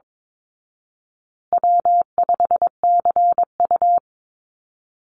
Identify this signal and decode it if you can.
Morse code, heard as a keyed tone, about 22 wpm, 700 Hz. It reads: W5CU
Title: Call Signs Repeated 2x - 22wpm